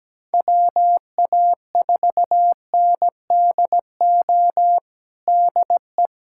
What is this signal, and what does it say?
Morse code, heard as a keyed tone, about 17 wpm, 700 Hz. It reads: WA4NDO DE
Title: USA Callsign DE Callsign 17wpm